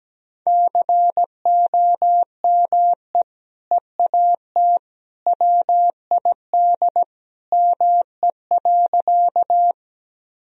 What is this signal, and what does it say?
Morse code, heard as a keyed tone, about 17 wpm, 700 Hz. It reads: COME EAT WID ME.
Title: Sentences from Top 300 Words and Abbreviations 17wpm